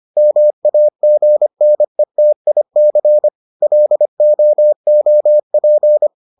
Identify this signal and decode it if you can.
Morse code, heard as a keyed tone, about 25 wpm, 600 Hz. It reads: MAGNETIC LOOP